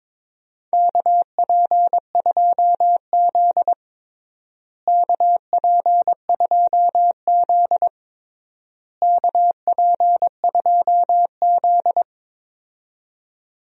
Morse code, heard as a keyed tone, about 22 wpm, 700 Hz. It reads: KP2Z KP2Z KP2Z
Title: DXCC-100 Call Signs - 4 Characters - Repeated 3x - 22wpm